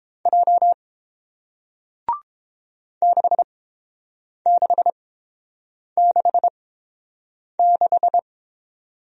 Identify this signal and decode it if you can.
Morse code, heard as a keyed tone, about 30 wpm, spaced as for 7 wpm, 700 Hz. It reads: J 6666